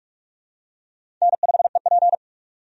Morse code, heard as a keyed tone, about 46 wpm, 700 Hz. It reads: N5EP
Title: USA Call Signs - 4 Characters - 45wpm